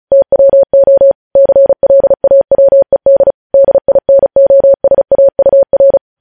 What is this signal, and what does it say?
Morse code, heard as a keyed tone, about 36 wpm, 550 Hz. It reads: TWO CLAWED DINOSAUR
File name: podcast / News headlines-1x-35x35